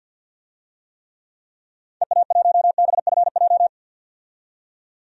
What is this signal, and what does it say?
Morse code, heard as a keyed tone, about 48 wpm, 700 Hz. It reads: EA1BFJ
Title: DXCC-100 Call Signs - 6 Characters - 50wpm